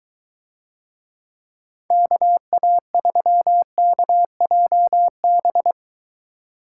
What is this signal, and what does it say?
Morse code, heard as a keyed tone, about 20 wpm, 700 Hz. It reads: KA3KJB